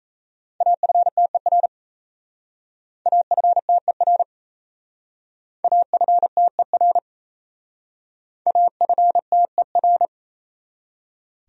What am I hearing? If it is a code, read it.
Morse code, heard as a keyed tone, about 35 wpm, 700 Hz. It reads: AFTER AFTER AFTER AFTER
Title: Speed-Racing Top 500 Words 28wpm